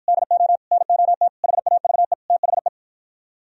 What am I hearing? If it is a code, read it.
Morse code, heard as a keyed tone, about 52 wpm, 700 Hz. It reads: DO NOT HAVE THE